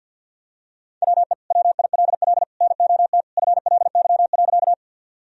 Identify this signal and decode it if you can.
Morse code, heard as a keyed tone, about 53 wpm, 700 Hz. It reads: WE WILL NOT FLY.
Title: Sentences from Top 1000 Words - Review Part 2 - 50wpm